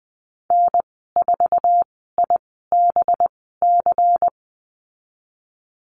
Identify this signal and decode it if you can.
Morse code, heard as a keyed tone, about 20 wpm, spaced as for 14 wpm, 700 Hz. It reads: N4IBC
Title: 193 Lesson 31g - Letter v - Call Signs 20wpm